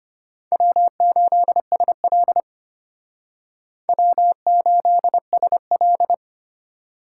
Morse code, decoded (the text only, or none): W8SL W8SL